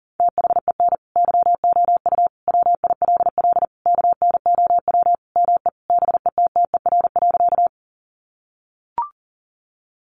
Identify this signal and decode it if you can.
Morse code, heard as a keyed tone, about 40 wpm, 700 Hz. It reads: THEN YOU WILL KNOW ME BETTER.
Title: Sentences from Top 400 Words 40wpm